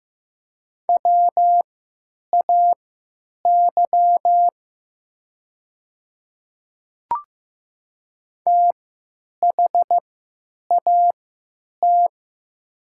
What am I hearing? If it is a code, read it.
Morse code, heard as a keyed tone, about 15 wpm, spaced as for 9 wpm, 700 Hz. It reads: WAY THAT
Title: 3x Character Spacing - Top 300 Words - 15wpm